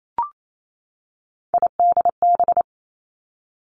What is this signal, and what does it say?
Morse code, heard as a keyed tone, about 28 wpm, 700 Hz. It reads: IDB
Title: Three Challenging Letter-Number Combinations 28wpm